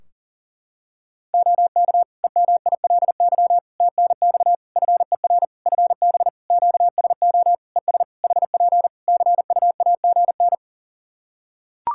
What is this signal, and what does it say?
Morse code, heard as a keyed tone, about 40 wpm, 700 Hz. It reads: OK EMILY TNX FER FB QSO ES HP CUAGN
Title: Practice QSO - Exchange 4 - Ending - 40wpm